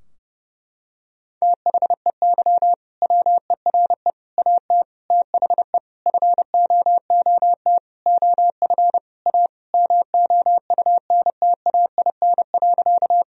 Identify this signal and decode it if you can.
Morse code, heard as a keyed tone, about 32 wpm, 700 Hz. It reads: THEY WERE AT THE FOOT OF A MOUNTAIN.
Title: Sentences from Top 1000 Words - Review Part 3 - 30wpm